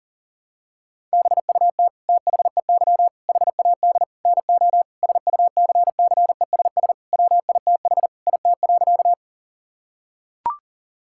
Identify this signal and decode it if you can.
Morse code, heard as a keyed tone, about 37 wpm, 700 Hz. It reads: BUT THEY HAD NO SUCCESS WITH IT.
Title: Sentences from Top 1000 Words - Review Part 6 - 40wpm